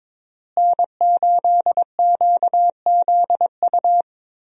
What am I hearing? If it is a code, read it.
Morse code, heard as a keyed tone, about 22 wpm, 700 Hz. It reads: N8QZU
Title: USA Call Signs - 5 Characters - Repeated 3x - 22wpm